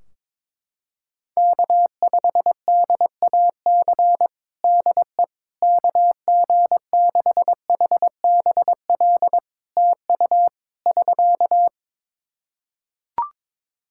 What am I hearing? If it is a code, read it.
Morse code, heard as a keyed tone, about 22 wpm, 700 Hz. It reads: K5DAC DE KG6HBL TU <SK>
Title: K1USN SST QSO Practice 22wpm